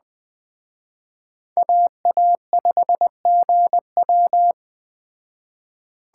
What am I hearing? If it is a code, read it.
Morse code, heard as a keyed tone, about 20 wpm, 700 Hz. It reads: AA5GW